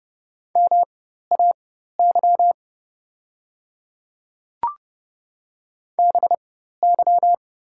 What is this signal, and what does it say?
Morse code, heard as a keyed tone, about 30 wpm, spaced as for 14 wpm, 700 Hz. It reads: MAY BY
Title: 4x Character Spacing - Top 200 Words - 30wpm